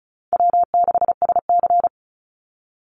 Morse code, heard as a keyed tone, about 34 wpm, 700 Hz. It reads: W6SC
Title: Easy Call Signs Repeated 3x - 35wpm